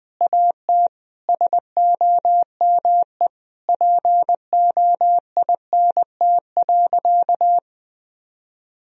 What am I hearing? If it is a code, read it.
Morse code, heard as a keyed tone, about 20 wpm, 700 Hz. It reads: AT SOME POINT.
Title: Sentences from Top 1000 Words - Review Part 1 - 20wpm